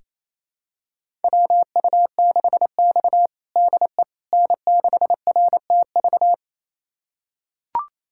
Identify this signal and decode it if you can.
Morse code, heard as a keyed tone, about 28 wpm, 700 Hz. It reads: WU6X DE N6RTV
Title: USA Callsign DE Callsign 28wpm